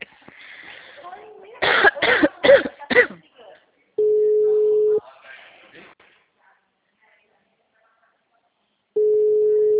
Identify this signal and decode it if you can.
Ringback tone (Europe, 425 Hz), with speech.